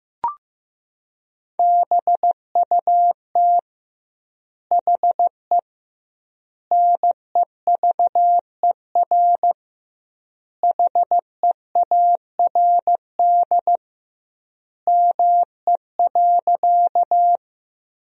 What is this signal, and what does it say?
Morse code, heard as a keyed tone, about 16 wpm, 700 Hz. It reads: BUT HE NEVER HEARD ME.
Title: 2x Word-Spacing - Sentences from Top 400 Words - 15wpm